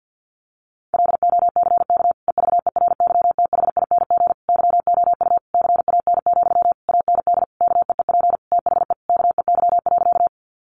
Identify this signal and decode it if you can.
Morse code, heard as a keyed tone, about 50 wpm, 700 Hz. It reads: LOCK EVERYTHING YOU CAN, AND KEEP THE KEY.
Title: Sentences from Top 1300 Words 50wpm